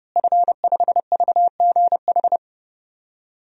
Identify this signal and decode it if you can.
Morse code, heard as a keyed tone, about 30 wpm, 700 Hz. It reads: F5VGH
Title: DXCC-100 Call Signs - 5 Characters - Repeated 3x - 30wpm